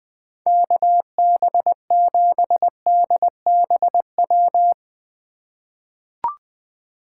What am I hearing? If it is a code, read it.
Morse code, heard as a keyed tone, about 20 wpm, 700 Hz. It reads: KB7DBW
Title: USA Call Signs - 6 Characters - 4x Character Spacing - 20wpm